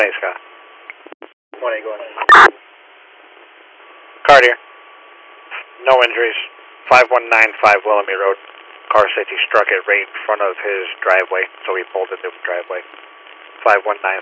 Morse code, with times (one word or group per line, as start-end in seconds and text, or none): none